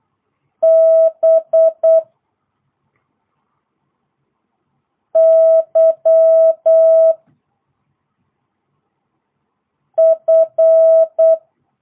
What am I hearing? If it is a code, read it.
Morse code, heard as a keyed tone, about 8 wpm, 650 Hz. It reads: B Y F